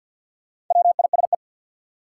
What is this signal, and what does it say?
Morse code, heard as a keyed tone, about 53 wpm, 700 Hz. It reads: WISE
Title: Sentences from Top 1600 Words - New Word Drill - 50wpm